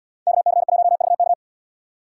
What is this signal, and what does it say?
Morse code, heard as a keyed tone, about 71 wpm, 700 Hz. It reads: KC1VK